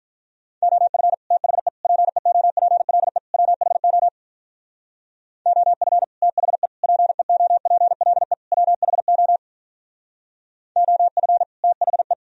Morse code, heard as a keyed tone, about 42 wpm, 700 Hz. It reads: OF THE PEOPLE WHO OF THE PEOPLE WHO OF THE
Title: Speed-Racing - Sets of 4 Words 35wpm